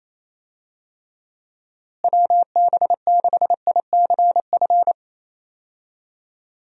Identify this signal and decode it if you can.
Morse code, heard as a keyed tone, about 27 wpm, 700 Hz. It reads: WB6ICF
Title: USA Call Signs - 6 Characters - Repeated 2x - 28wpm